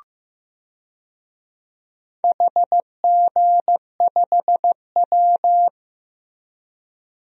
Morse code, heard as a keyed tone, about 15 wpm, 700 Hz. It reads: HG5W
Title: DXCC-100 Call Signs - 4 Characters - Repeated 2x - 15wpm